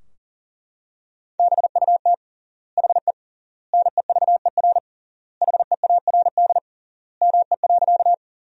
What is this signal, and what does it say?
Morse code, heard as a keyed tone, about 40 wpm, 700 Hz. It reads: BUT HE NEVER HEARD ME.
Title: 3x Word-Spacing - Sentences from Top 400 Words - 40wpm